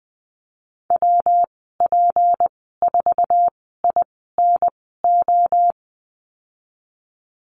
Morse code, heard as a keyed tone, about 20 wpm, spaced as for 14 wpm, 700 Hz. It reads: WP4INO
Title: 167 Lesson 27g - Number 7 - Call Signs 20wpm